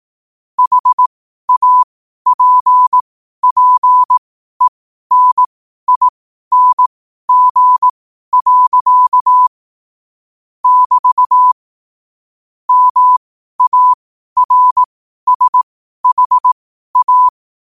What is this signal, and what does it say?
Morse code, heard as a keyed tone, about 18 wpm, spaced as for 12 wpm, 1000 Hz. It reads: HAPPENING. <BT> MARSHA
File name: qotdfiles / 12wpm